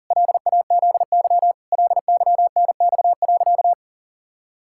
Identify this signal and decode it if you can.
Morse code, heard as a keyed tone, about 40 wpm, 700 Hz. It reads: LAZY LYNX.
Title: Pangram Challenge v2 - 40wpm